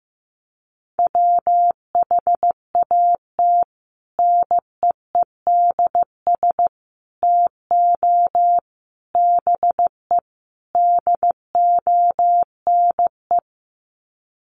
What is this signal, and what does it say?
Morse code, heard as a keyed tone, about 15 wpm, 700 Hz. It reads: WHAT NEEDS TO BE DONE
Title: Sets of 5 Words Encore 15wpm